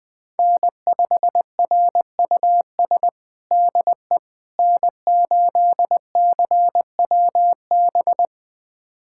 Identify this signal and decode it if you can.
Morse code, heard as a keyed tone, about 20 wpm, 700 Hz. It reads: N5RUS DE N8CWB
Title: USA Callsign DE Callsign - 5 Characters - 20wpm